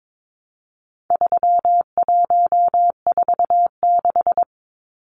Morse code, heard as a keyed tone, about 22 wpm, 700 Hz. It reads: 3146
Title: Numbers - 4-Digit 22wpm